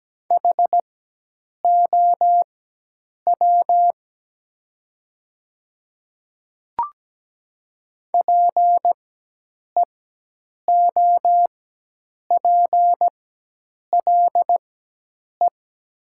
Morse code, heard as a keyed tone, about 17 wpm, spaced as for 8 wpm, 700 Hz. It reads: HOW PEOPLE
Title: 4x Character Spacing - Top 100 Words - 17wpm